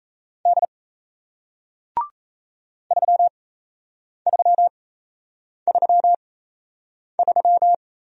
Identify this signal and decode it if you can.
Morse code, heard as a keyed tone, about 37 wpm, spaced as for 8 wpm, 700 Hz. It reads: D 3333